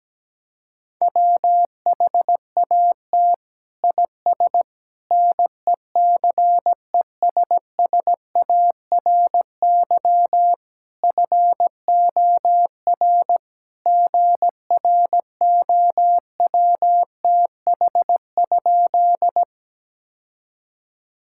Morse code, heard as a keyed tone, about 17 wpm, 700 Hz. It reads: WHAT IS NECESSARY FOR GROWTH?
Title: Sentences from Top 1000 Words - Review Part 5 - 17wpm